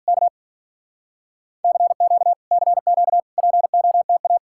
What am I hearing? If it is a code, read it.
Morse code, heard as a keyed tone, about 53 wpm, 700 Hz. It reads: K CQ CQ POTA